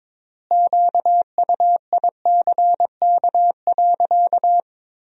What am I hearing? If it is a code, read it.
Morse code, heard as a keyed tone, about 21 wpm, 700 Hz. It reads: QUICK.
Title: 4x Word-Spacing - Sentences from Top 500 Words - 22wpm